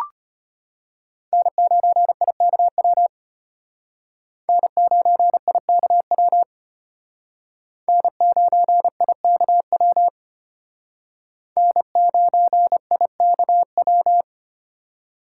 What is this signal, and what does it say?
Morse code, heard as a keyed tone, about 32 wpm, 700 Hz. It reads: N9IKW N9IKW N9IKW N9IKW